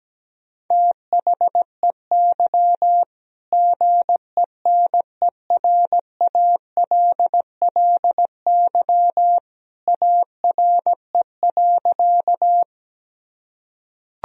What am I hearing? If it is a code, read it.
Morse code, heard as a keyed tone, about 17 wpm, 700 Hz. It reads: THEY GENERALLY ARE.
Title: Sentences from Top 1000 Words - Review Part 4 - 17wpm